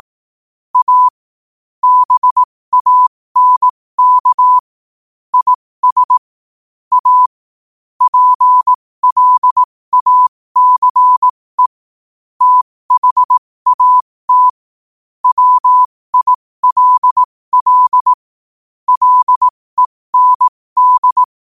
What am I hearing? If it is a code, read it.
Morse code, heard as a keyed tone, about 18 wpm, spaced as for 15 wpm, 1000 Hz. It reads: A BANK IS A PLACE THAT WILL LEND
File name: qotdfiles / 15wpm